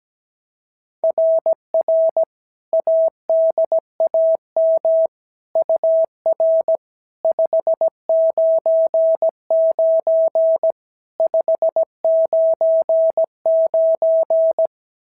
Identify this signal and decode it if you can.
Morse code, heard as a keyed tone, about 16 wpm, 650 Hz. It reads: RR ADAM UR 599 599
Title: SOTA QSO Practice 17wpm